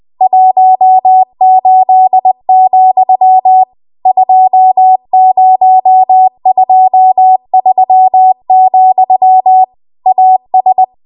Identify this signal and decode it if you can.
Morse code, heard as a keyed tone, about 20 wpm, 750 Hz. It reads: 18, 2023, AS
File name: Morse / 20WPM